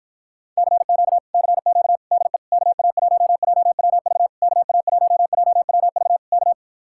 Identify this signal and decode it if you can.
Morse code, heard as a keyed tone, about 53 wpm, 700 Hz. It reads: CQ CQ DE KA1JWV KA1JWV K